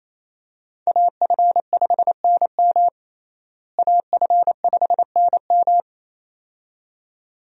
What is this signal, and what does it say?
Morse code, heard as a keyed tone, about 28 wpm, 700 Hz. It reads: AF5NM AF5NM
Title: USA Call Signs - 5 Characters - Repeated 2x - 28wpm